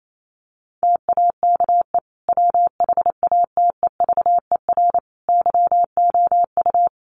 Morse code, heard as a keyed tone, about 28 wpm, 700 Hz. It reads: TAKE WHATEVER YOU
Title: Sentences from Top 1100 Words 28wpm